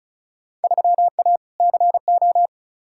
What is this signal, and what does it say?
Morse code, heard as a keyed tone, about 36 wpm, 690 Hz. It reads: 3A CO
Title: Field Day Exchanges Encore - 35wpm